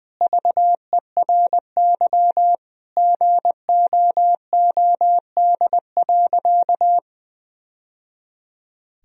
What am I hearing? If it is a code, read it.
Morse code, heard as a keyed tone, about 21 wpm, 700 Hz. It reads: VERY GOOD.
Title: Sentences from Top 1000 Words - Review Part 1 - 20wpm